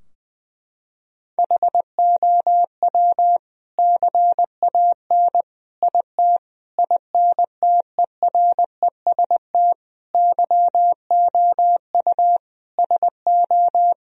Morse code, heard as a keyed tone, about 20 wpm, 700 Hz. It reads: HOW CAN IT INTEREST YOU SO
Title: Sentences from Top 1000 Words - Review Part 2 - 20wpm